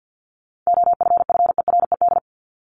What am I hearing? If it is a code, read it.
Morse code, heard as a keyed tone, about 50 wpm, 700 Hz. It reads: OFFERED
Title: ED Suffix - Top 100 Words with Base Words 50wpm